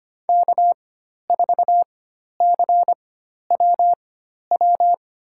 Morse code, heard as a keyed tone, about 25 wpm, spaced as for 12 wpm, 700 Hz. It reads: K4CWW
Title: USA Call Signs - 5 Characters - 4x Character Spacing - 25wpm